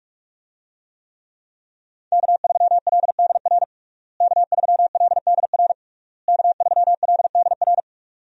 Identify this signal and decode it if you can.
Morse code, heard as a keyed tone, about 42 wpm, 700 Hz. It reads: K3LDR K3LDR K3LDR
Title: USA Call Signs - 5 Characters - Repeated 3x - 45wpm